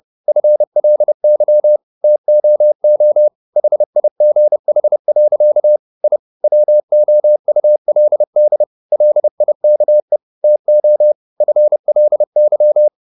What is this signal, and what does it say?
Morse code, heard as a keyed tone, about 30 wpm, 600 Hz. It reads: FLY TOO HIGH. I WOULD LIKE TO FLY